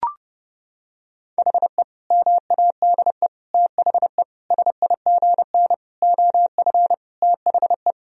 Morse code, heard as a keyed tone, about 30 wpm, 700 Hz. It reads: HE MADE THE SIGN OF THE
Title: Sentences from Top 1000 Words - Review Part 1 - 30wpm